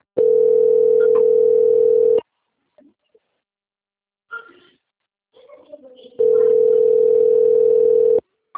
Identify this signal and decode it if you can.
Ringback tone (North America), 440 + 480 Hz, with speech.
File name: out- - - - - .15183